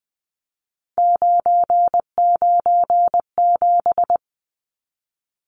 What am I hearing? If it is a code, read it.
Morse code, heard as a keyed tone, about 20 wpm, 700 Hz. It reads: 997
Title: Numbers - 3-Digit 20wpm